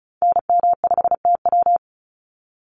Morse code, heard as a keyed tone, about 34 wpm, 700 Hz. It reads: NM5TW
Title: NA SOTA Activator Callsigns 35wpm